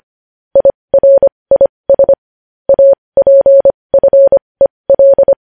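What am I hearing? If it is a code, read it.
Morse code, heard as a keyed tone, about 25 wpm, spaced as for 20 wpm, 550 Hz. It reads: IRIS APFEL
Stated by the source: Morse news headlines sent at 20, opening with vvv.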